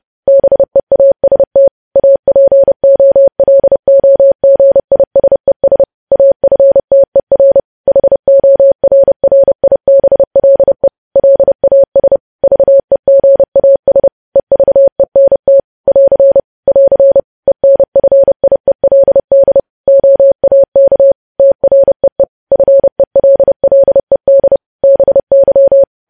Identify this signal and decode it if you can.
Morse code, heard as a keyed tone, about 30 wpm, 550 Hz. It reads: BEAST APOLOGISES AFTER HORRIBLE LAS VEGAS EVENT <AR> <AR> ENFIELD OAK TREE FELLED BY
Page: news headlines in morse code at 30 wpm